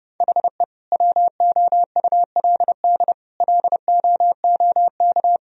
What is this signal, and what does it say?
Morse code, heard as a keyed tone, about 30 wpm, 700 Hz. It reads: HE WOULD LOOK
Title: Sentences from Top 1000 Words - Review Part 3 - 30wpm